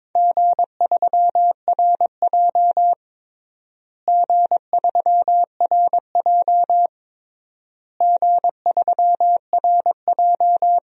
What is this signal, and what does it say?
Morse code, heard as a keyed tone, about 22 wpm, 700 Hz. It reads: G3RJ G3RJ G3RJ
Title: DXCC-100 Call Signs - 4 Characters - Repeated 3x - 22wpm